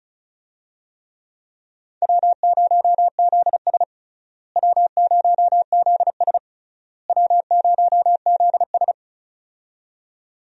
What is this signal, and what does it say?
Morse code, heard as a keyed tone, about 34 wpm, 700 Hz. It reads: W0ZS W0ZS W0ZS
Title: USA Call Signs - 4 Characters - Repeated 3x - 35wpm